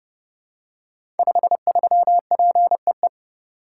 Morse code, heard as a keyed tone, about 28 wpm, 700 Hz. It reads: 53PEE
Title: Code Groups - 5 Letter:Numbers - 30wpm